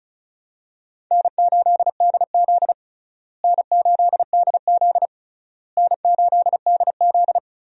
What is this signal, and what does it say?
Morse code, heard as a keyed tone, about 36 wpm, 700 Hz. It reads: N8DZ N8DZ N8DZ
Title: USA Call Signs - 4 Characters - Repeated 3x - 35wpm